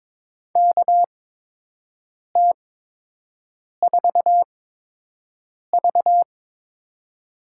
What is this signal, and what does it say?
Morse code, heard as a keyed tone, about 22 wpm, 700 Hz. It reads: K T 4 V
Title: USA Call Signs - 5 Characters - 8x Character Spacing - 22wpm